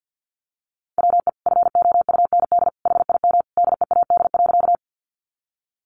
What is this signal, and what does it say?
Morse code, heard as a keyed tone, about 50 wpm, 700 Hz. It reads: WE FOUND HIM DEAD.
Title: Sentences from Top 1500 Words 50wpm